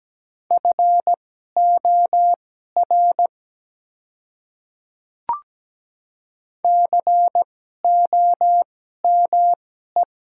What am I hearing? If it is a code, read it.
Morse code, heard as a keyed tone, about 17 wpm, spaced as for 12 wpm, 700 Hz. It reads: FOR COME
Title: 2x Character Spacing - Top 100 Words - 17wpm